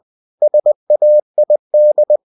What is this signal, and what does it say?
Morse code, heard as a keyed tone, about 20 wpm, 600 Hz. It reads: SAID
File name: cw-books / winnie-the-pooh / winnie-20wpm-page-0026